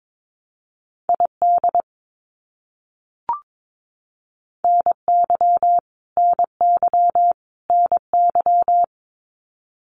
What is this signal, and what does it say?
Morse code, heard as a keyed tone, about 22 wpm, 700 Hz. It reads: ID NY NY NY
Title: US States Repeated 3x 22wpm